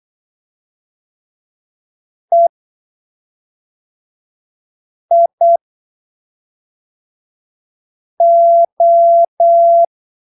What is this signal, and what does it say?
Morse code, heard as a keyed tone, about 8 wpm, spaced as for 3 wpm, 680 Hz. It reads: EIO
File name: arquivos / texto / texto 18 08ppm